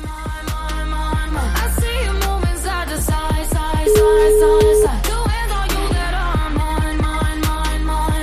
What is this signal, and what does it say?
Ringback tone (Europe, 425 Hz), with noise.